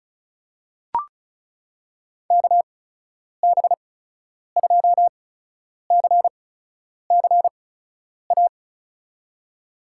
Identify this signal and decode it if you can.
Morse code, heard as a keyed tone, about 36 wpm, spaced as for 10 wpm, 700 Hz. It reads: KB2CCA
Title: USA Call Signs - 6 Characters - 8x Character Spacing - 35wpm